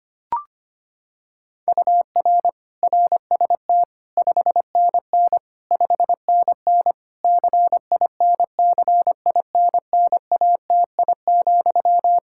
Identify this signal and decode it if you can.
Morse code, heard as a keyed tone, about 25 wpm, 700 Hz. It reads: UR RST 5NN 5NN CINCINNATI,
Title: Practice QSO - Exchange 1 - Beginning - 25wpm